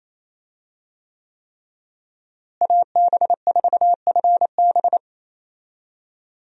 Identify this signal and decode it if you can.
Morse code, heard as a keyed tone, about 28 wpm, 700 Hz. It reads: AB4FB
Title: USA Call Signs - 5 Characters - Repeated 3x - 28wpm